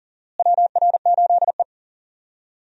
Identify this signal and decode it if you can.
Morse code, heard as a keyed tone, about 40 wpm, 700 Hz. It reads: WR8E
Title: USA Call Signs - 4 Characters - Repeated 3x - 40wpm